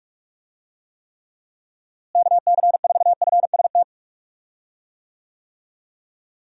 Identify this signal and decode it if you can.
Morse code, heard as a keyed tone, about 46 wpm, 700 Hz. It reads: KC4FST
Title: USA Call Signs - 6 Characters - Repeated 2x - 45wpm